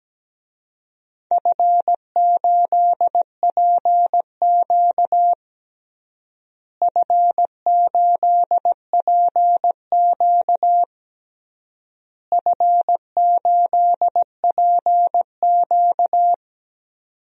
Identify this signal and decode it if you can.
Morse code, heard as a keyed tone, about 17 wpm, 700 Hz. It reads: F8PQ F8PQ F8PQ
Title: DXCC-100 Call Signs - 4 Characters - Repeated 3x - 17wpm